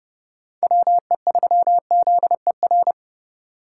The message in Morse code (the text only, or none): WE3ZER